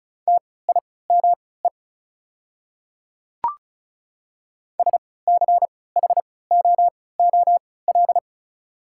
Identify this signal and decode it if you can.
Morse code, heard as a keyed tone, about 34 wpm, spaced as for 20 wpm, 690 Hz. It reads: TIME SCHOOL
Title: 3x Character Spacing - Top 300 Words - 35wpm